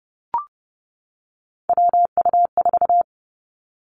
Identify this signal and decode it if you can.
Morse code, heard as a keyed tone, about 31 wpm, 700 Hz. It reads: WU4
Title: Three Challenging Letter-Number Combinations 30wpm